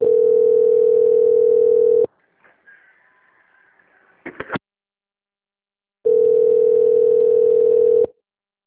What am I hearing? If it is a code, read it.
Ringback tone (North America), 440 + 480 Hz.
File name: out- - - - - .47618